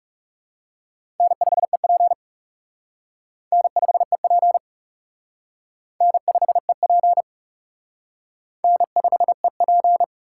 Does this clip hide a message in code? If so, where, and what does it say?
Yes: Morse code from 1.2 to 10.0 s, heard as a keyed tone, about 38 wpm, 700 Hz. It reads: N5EP N5EP N5EP N5EP